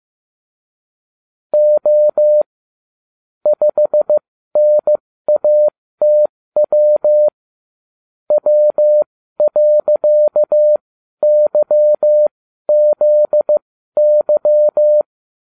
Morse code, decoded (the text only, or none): O 5NATW W.YZY